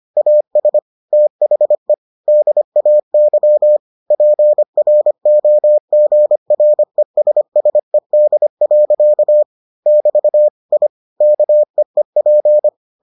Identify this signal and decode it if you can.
Morse code, heard as a keyed tone, about 25 wpm, 600 Hz. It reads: AS THE DAY PROGRESSED. <BT> I KEEP